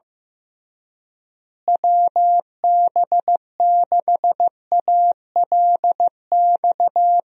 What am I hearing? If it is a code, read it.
Morse code, heard as a keyed tone, about 15 wpm, 700 Hz. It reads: WB6ALX
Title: USA Call Signs - 6 Characters - Repeated 3x - 15wpm